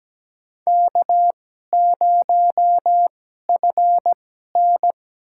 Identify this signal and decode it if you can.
Morse code, heard as a keyed tone, about 17 wpm, spaced as for 12 wpm, 700 Hz. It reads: K0FN
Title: USA Call Signs - 2x Character Spacing - 17wpm